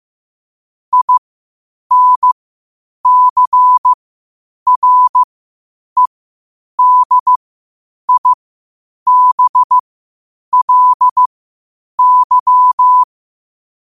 Morse code, heard as a keyed tone, about 15 wpm, spaced as for 8 wpm, 1000 Hz. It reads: INCREDIBLY
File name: 8wpm